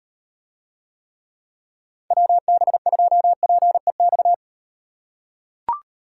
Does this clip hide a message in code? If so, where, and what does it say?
Yes: Morse code from 2.1 to 4.3 s, heard as a keyed tone, about 37 wpm, 700 Hz. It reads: WB2PEX